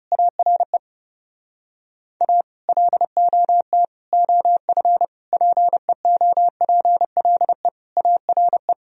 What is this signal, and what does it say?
Morse code, heard as a keyed tone, about 28 wpm, 700 Hz. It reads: ARE A LOT OF PEOPLE ARE
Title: Speed-Racing - Sets of 5 Words 30wpm